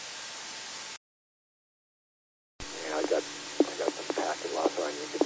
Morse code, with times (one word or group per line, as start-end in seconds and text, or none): none